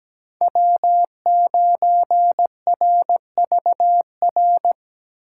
Morse code, heard as a keyed tone, about 17 wpm, 700 Hz. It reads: W9RVR
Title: USA Call Signs - 6x Character Spacing - 17wpm